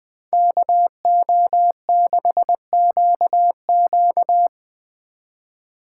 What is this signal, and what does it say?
Morse code, heard as a keyed tone, about 22 wpm, 700 Hz. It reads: KO6QQ
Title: Speed Racing - USA 2x2 Call Signs 20wpm